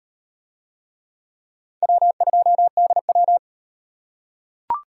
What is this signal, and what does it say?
Morse code, heard as a keyed tone, about 38 wpm, 700 Hz. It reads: W2DW